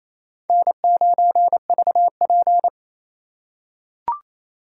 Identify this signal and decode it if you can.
Morse code, heard as a keyed tone, about 28 wpm, 700 Hz. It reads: N9VP
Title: USA Call Signs - 3x Character Spacing - 28wpm